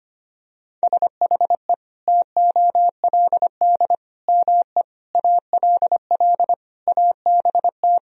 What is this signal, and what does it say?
Morse code, heard as a keyed tone, about 25 wpm, 700 Hz. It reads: SHE TOLD ME ALL ABT
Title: Sentences from Top 400 Words and Abbreviations 25wpm